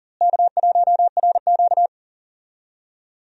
Morse code, heard as a keyed tone, about 40 wpm, 700 Hz. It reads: K1RQ
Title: USA Call Signs - 4 Characters - Repeated 2x - 40wpm